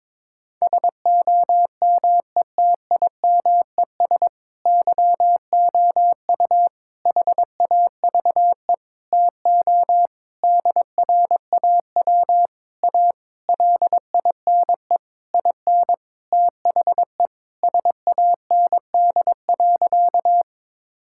Morse code, heard as a keyed tone, about 22 wpm, 700 Hz. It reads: SOMETIMES YOU HAVE TO DRAW A LINE IN THE SAND.
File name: Sentences with 7-Word Phrases - 22wpm